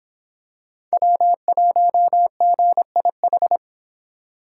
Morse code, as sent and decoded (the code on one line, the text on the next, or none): .-- .---- --. .. ....
W1GIH